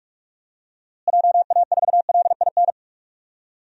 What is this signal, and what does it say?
Morse code, heard as a keyed tone, about 44 wpm, 700 Hz. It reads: JA4LIN
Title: DXCC-100 Call Signs - 6 Characters - Repeated 2x - 45wpm